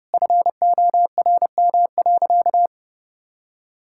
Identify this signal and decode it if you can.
Morse code, heard as a keyed tone, about 30 wpm, 700 Hz. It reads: FORM.
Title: Sentences from Top 1000 Words - Review Part 5 - 30wpm